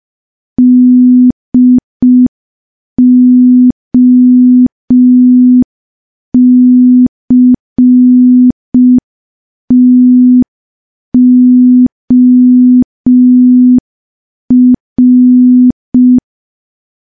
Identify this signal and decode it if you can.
Morse code, heard as a keyed tone, about 5 wpm, 250 Hz. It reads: DOCTOR
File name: morse / morse-training-05WPM